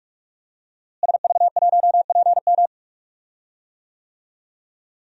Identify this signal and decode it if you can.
Morse code, heard as a keyed tone, about 46 wpm, 700 Hz. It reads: SV1PM
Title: DXCC-100 Call Signs - 5 Characters - Repeated 3x - 45wpm